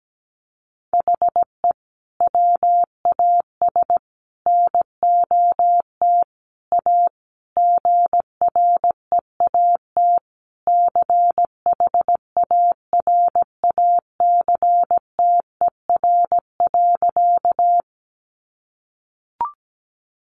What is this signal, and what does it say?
Morse code, heard as a keyed tone, about 17 wpm, 700 Hz. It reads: HE WAS NOT A GREAT CHARACTER.
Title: Sentences from Top 1200 Words 17wpm